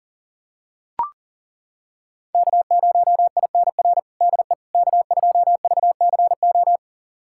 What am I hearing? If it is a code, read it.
Morse code, heard as a keyed tone, about 37 wpm, 700 Hz. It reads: K0INR DE K2VCO
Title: USA Callsign DE Callsign - 5 Characters - 40wpm